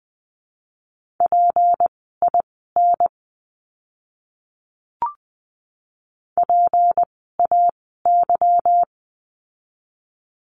Morse code, heard as a keyed tone, about 20 wpm, spaced as for 15 wpm, 700 Hz. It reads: PIN PAY
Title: 153 Lesson 25f - Letter p - Words 20wpm